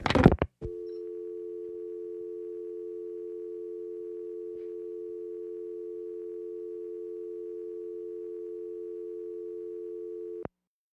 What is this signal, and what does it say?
Dial tone (North America), 350 + 440 Hz.
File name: functions / downloadSound